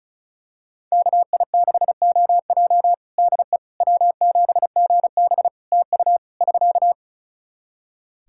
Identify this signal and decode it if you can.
Morse code, heard as a keyed tone, about 34 wpm, 700 Hz. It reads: KI6OJ DE W7GB TU <SK>
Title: Practice QSO - Exchange 4 - Ending - 35wpm